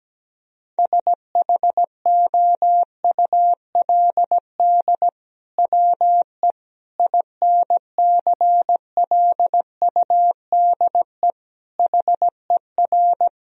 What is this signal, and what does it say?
Morse code, heard as a keyed tone, about 17 wpm, 700 Hz. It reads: SHOULD WE INCLUDE HER
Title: Sentences from Top 1000 Words - Review Part 3 - 17wpm